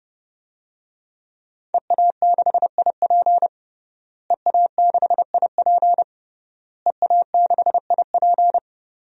Morse code, heard as a keyed tone, about 30 wpm, 700 Hz. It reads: EA6IP EA6IP EA6IP
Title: DXCC-100 Call Signs - 5 Characters - Repeated 3x - 30wpm